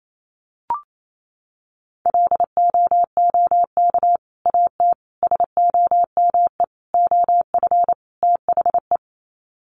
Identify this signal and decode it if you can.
Morse code, heard as a keyed tone, about 28 wpm, 700 Hz. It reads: LOOK AT SOME OF THE
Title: Sets of 5 Words Encore 28wpm